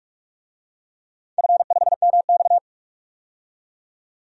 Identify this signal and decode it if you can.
Morse code, heard as a keyed tone, about 43 wpm, 700 Hz. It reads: F5MX
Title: DXCC-100 Call Signs - 4 Characters - 45wpm